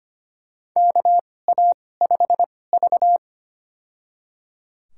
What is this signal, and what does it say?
Morse code, heard as a keyed tone, about 25 wpm, spaced as for 18 wpm, 700 Hz. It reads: KA5V
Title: USA Call Signs - 4 Characters - 2x Character Spacing - 25wpm